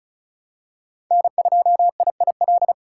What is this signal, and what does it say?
Morse code, heard as a keyed tone, about 36 wpm, 700 Hz. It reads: N2IIL
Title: USA Call Signs - 5 Characters - 4x Character Spacing - 35wpm